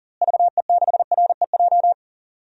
Morse code, heard as a keyed tone, about 40 wpm, 700 Hz. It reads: VE6REJ
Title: DXCC-100 Call Signs - 6 Characters - Repeated 3x - 40wpm